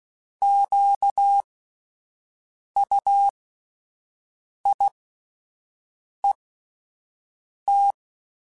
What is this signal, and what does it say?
Morse code, heard as a keyed tone, about 16 wpm, spaced as for 5 wpm, 800 Hz. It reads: QUIET